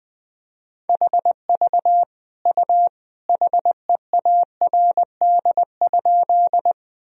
Morse code, heard as a keyed tone, about 20 wpm, 700 Hz. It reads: HV U HEARD?
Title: Sentences from Top 400 Words and Abbreviations 20wpm